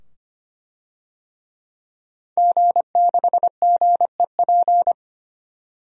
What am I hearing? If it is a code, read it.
Morse code, heard as a keyed tone, about 25 wpm, 700 Hz. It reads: G6GEP
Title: DXCC-100 Call Signs - 5 Characters - 25wpm